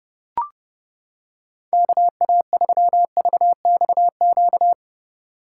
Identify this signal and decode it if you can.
Morse code, heard as a keyed tone, about 29 wpm, 700 Hz. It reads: KA3VXQ